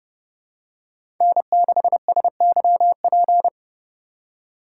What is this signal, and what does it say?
Morse code, heard as a keyed tone, about 30 wpm, 700 Hz. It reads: N6SYP
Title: USA Call Signs - 5 Characters - 3x Character Spacing - 30wpm